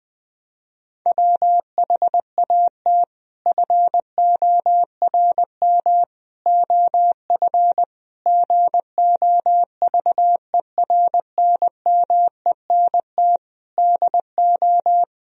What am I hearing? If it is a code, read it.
Morse code, heard as a keyed tone, about 20 wpm, 700 Hz. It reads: WHAT FORM OF GOVERNMENT DO
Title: Sentences from Top 1000 Words - Review Part 3 - 20wpm